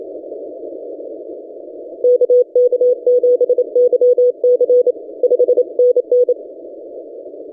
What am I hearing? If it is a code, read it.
Morse code, heard as a keyed tone, about 27 wpm, 490 Hz. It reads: KK7YC 5NN
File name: ft5zm-17cw-2014-01-27-0120z-lp